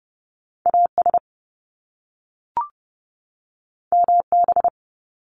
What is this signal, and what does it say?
Morse code, heard as a keyed tone, about 30 wpm, 700 Hz. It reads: AS MB
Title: Top Two Letter Combinations in Words 30wpm